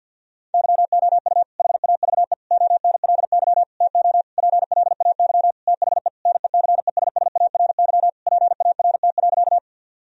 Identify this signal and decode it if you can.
Morse code, heard as a keyed tone, about 51 wpm, 700 Hz. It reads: YOU HAVE ONLY TO PLAY THE NECESSARY PART.
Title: Sentences from Top 1000 Words - Review Part 1 - 50wpm